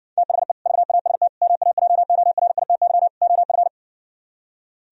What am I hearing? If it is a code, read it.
Morse code, heard as a keyed tone, about 70 wpm, 700 Hz. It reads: THE VAST MAJORITY OF